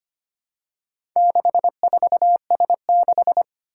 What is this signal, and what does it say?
Morse code, heard as a keyed tone, about 25 wpm, 700 Hz. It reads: 64S6
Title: Four Challenging Letter-Number Combinations 25wpm